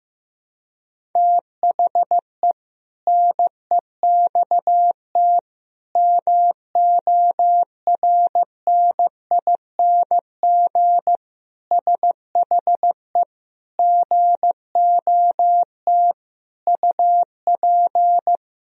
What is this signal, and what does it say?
Morse code, heard as a keyed tone, about 15 wpm, 700 Hz. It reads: THE NEXT MORNING SHE GOT UP
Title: Sentences from Top 1000 Words - Review Part 2 - 15wpm